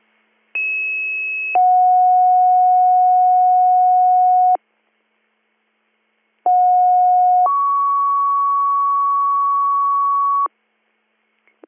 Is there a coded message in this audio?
Yes: two-tone page from 0.5 to 10.5 s, heard as a one-second tone, then a three-second tone, about 2600 then 730 Hz, about 730 then 1120 Hz.